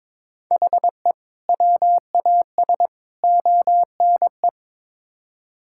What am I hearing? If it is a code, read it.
Morse code, heard as a keyed tone, about 21 wpm, 700 Hz. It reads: HE WAS ONE
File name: practice / 22wpm / Build A Phrase - 5-Word Phrases - 22wpm